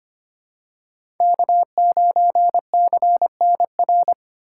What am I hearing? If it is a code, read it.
Morse code, heard as a keyed tone, about 25 wpm, 700 Hz. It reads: K9CNR
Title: USA Call Signs - 5 Characters - 2x Character Spacing - 25wpm